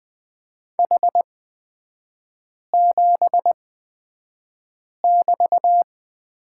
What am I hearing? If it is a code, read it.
Morse code, heard as a keyed tone, about 20 wpm, spaced as for 5 wpm, 700 Hz. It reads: H7<BT>
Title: 243 Lesson 39e - Letter z - Three Characters 20wpm